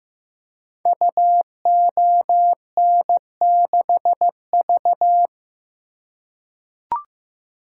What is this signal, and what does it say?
Morse code, heard as a keyed tone, about 15 wpm, 700 Hz. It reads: UON6V
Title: Code Groups - 5 Letter:Numbers - 15wpm